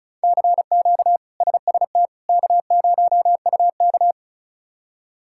Morse code, heard as a keyed tone, about 34 wpm, 700 Hz. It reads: CQ SST K0UK
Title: K1USN SST QSO Practice - 35wpm